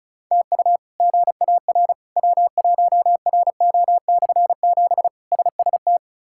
Morse code, heard as a keyed tone, about 34 wpm, 700 Hz. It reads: TU GAR W1RO/7 SST
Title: K1USN SST QSO Practice - 35wpm